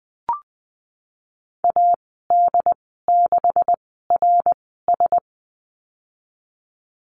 Morse code, heard as a keyed tone, about 20 wpm, spaced as for 15 wpm, 700 Hz. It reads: AD6RS
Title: 147 Lesson 24g - Letter y - Call Signs 20wpm